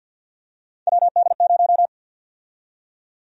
Morse code, heard as a keyed tone, about 53 wpm, 700 Hz. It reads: WD0
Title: USA Call Sign Prefixes - 50wpm